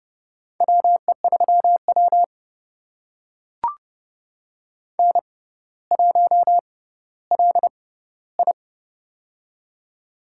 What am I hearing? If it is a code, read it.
Morse code, heard as a keyed tone, about 32 wpm, spaced as for 16 wpm, 700 Hz. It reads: WE3W N1LI
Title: USA Call Signs - 4 Characters - 6x Character Spacing - 30wpm